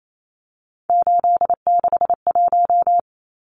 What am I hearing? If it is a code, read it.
Morse code, heard as a keyed tone, about 28 wpm, 700 Hz. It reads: 861
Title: Numbers - 3-Digit 28wpm